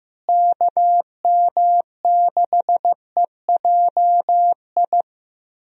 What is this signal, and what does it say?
Morse code, heard as a keyed tone, about 15 wpm, 700 Hz. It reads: KM6EJI
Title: USA Call Signs - 6 Characters - Repeated 3x - 15wpm